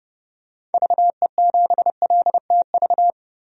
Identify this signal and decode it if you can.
Morse code, heard as a keyed tone, about 30 wpm, 700 Hz. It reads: VE7LTV
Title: DXCC-100 Call Signs - 6 Characters - 30wpm